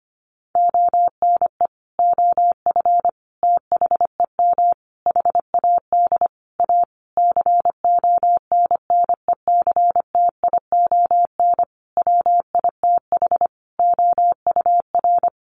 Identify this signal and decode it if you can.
Morse code, heard as a keyed tone, about 25 wpm, 700 Hz. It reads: ONE OF THEM HAD A CONNECTION WITH OUR
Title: Sentences from Top 1300 Words 25wpm